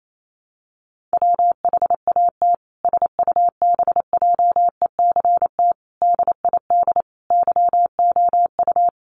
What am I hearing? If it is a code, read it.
Morse code, heard as a keyed tone, about 28 wpm, 700 Hz. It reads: WHAT SUBJECT DID YOU
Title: Sentences from Top 1500 Words 28wpm